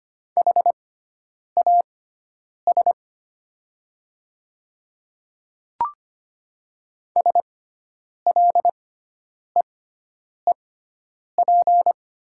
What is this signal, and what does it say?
Morse code, heard as a keyed tone, about 25 wpm, spaced as for 9 wpm, 700 Hz. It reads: HAS SLEEP
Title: 6x Character Spacing - Top 500 Words - 25wpm